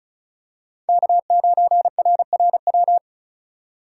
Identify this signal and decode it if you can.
Morse code, heard as a keyed tone, about 34 wpm, 700 Hz. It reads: K9RRW
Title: USA Call Signs - 5 Characters - 2x Character Spacing - 35wpm